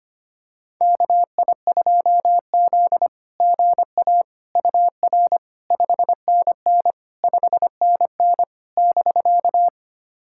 Morse code, heard as a keyed tone, about 25 wpm, 700 Hz. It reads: KI2Z GA UR 5NN 5NN <BK>
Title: SOTA QSO Practice 25wpm